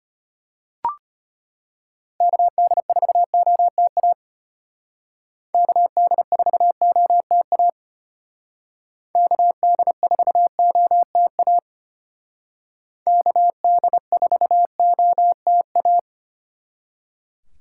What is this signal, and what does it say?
Morse code, heard as a keyed tone, about 32 wpm, 700 Hz. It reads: KD4OTA KD4OTA KD4OTA KD4OTA